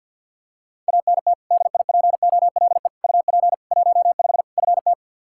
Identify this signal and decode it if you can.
Morse code, heard as a keyed tone, about 46 wpm, 700 Hz. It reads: ANT DIPOLE UP 15 FT
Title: Practice QSO - Exchange 2 - Info - 50wpm